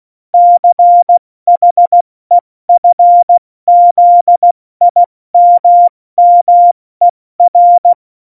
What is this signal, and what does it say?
Morse code, heard as a keyed tone, about 16 wpm, spaced as for 14 wpm, 700 Hz. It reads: CHEFZIMMER
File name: Morsekurs / Witze / Witz 26. Januar 2021, 70 F80